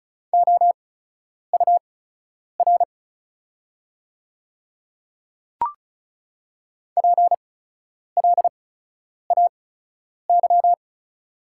Morse code, heard as a keyed tone, about 36 wpm, spaced as for 10 wpm, 690 Hz. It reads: OUR PLAY
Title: 8x Character Spacing - Top 200 Words - 35wpm